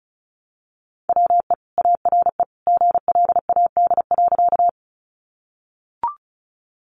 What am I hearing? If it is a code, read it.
Morse code, heard as a keyed tone, about 34 wpm, 700 Hz. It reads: WE ARE GLAD.
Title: Sentences from Top 800 Words 35wpm